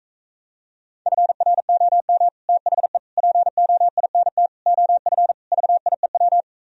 Morse code, heard as a keyed tone, about 45 wpm, 700 Hz. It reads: FROM THE POINT OF VIEW